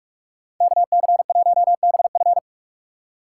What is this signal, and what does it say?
Morse code, heard as a keyed tone, about 42 wpm, 700 Hz. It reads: KC1BF